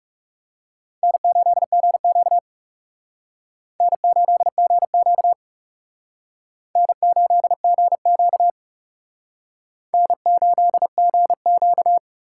Morse code, heard as a keyed tone, about 37 wpm, 700 Hz. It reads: N8GQ N8GQ N8GQ N8GQ
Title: Speed Racing - USA 1x2 Call Signs 30wpm